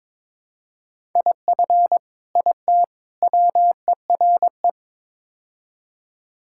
Morse code, heard as a keyed tone, about 22 wpm, 700 Hz. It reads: IF IT WERE
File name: practice / 22wpm / Build A Phrase - 5-Word Phrases - 22wpm